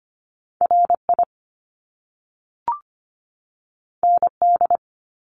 Morse code, heard as a keyed tone, about 25 wpm, 700 Hz. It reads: RI ND
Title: Top Two Letter Combinations in Words 25wpm